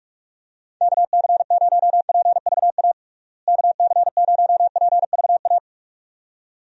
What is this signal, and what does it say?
Morse code, heard as a keyed tone, about 42 wpm, 700 Hz. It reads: KC0PVA KC0PVA
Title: USA Call Signs - 6 Characters - Repeated 2x - 45wpm